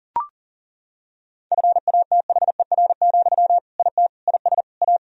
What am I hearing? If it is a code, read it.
Morse code, heard as a keyed tone, about 40 wpm, 700 Hz. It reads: FATHER, IT IS A
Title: Sentences from Top 1000 Words - Review Part 2 - 40wpm